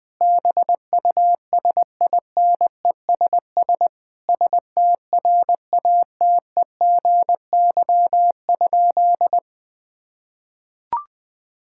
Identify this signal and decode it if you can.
Morse code, heard as a keyed tone, about 20 wpm, 700 Hz. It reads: BUSINESS STRATEGY?
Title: Sentences from Top 1000 Words - Review Part 6 - 20wpm